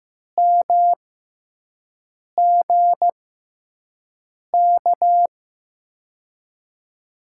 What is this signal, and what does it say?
Morse code, heard as a keyed tone, about 16 wpm, spaced as for 5 wpm, 700 Hz. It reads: MGK
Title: USA Call Signs - 6 Characters - 6x Character Spacing - 15wpm